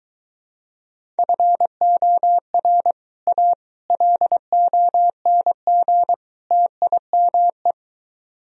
Morse code, heard as a keyed tone, about 23 wpm, 700 Hz. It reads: FOR A LONG TIME